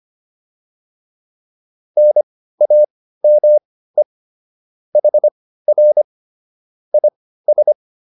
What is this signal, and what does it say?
Morse code, heard as a keyed tone, about 25 wpm, spaced as for 15 wpm, 600 Hz. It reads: NAME HR IS